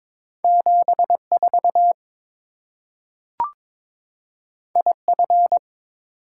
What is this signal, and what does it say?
Morse code, heard as a keyed tone, about 22 wpm, 690 Hz. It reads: 74 IF
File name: practice / 22wpm / Two Letter-Number Combinations - 22wpm